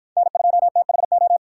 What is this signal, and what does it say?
Morse code, heard as a keyed tone, about 51 wpm, 700 Hz. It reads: N2THO
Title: Speed Racing - USA Call Signs 45wpm